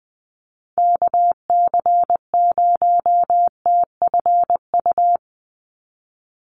Morse code, heard as a keyed tone, about 20 wpm, 700 Hz. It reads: KC0TFU
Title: USA Call Signs 20wpm